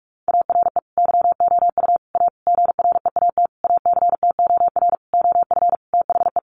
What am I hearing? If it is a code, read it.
Morse code, heard as a keyed tone, about 44 wpm, 700 Hz. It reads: ARE YOU A GREAT ACTOR OF THE
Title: Sentences from Top 1500 Words 45wpm